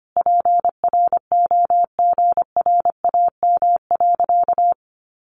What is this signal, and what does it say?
Morse code, heard as a keyed tone, about 25 wpm, 700 Hz. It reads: PROGRAM.
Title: Sentences from Top 1500 Words 25wpm